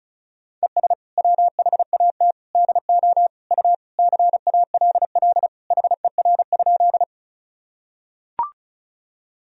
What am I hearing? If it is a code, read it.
Morse code, heard as a keyed tone, about 36 wpm, 700 Hz. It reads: ES WHAT DO U CALL HER?
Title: Sentences from Top 100 Words and Abbreviations 35wpm